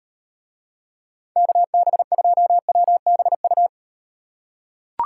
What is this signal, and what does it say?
Morse code, heard as a keyed tone, about 38 wpm, 700 Hz. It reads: KB2WBU